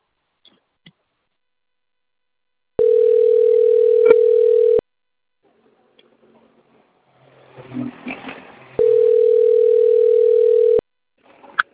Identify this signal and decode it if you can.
Ringback tone (North America), 440 + 480 Hz.